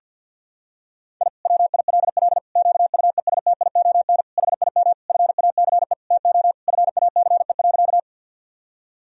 Morse code, heard as a keyed tone, about 50 wpm, 700 Hz. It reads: I WILL QUESTION HIM FACE TO FACE.
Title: Sentences from Top 1000 Words - Review Part 2 - 50wpm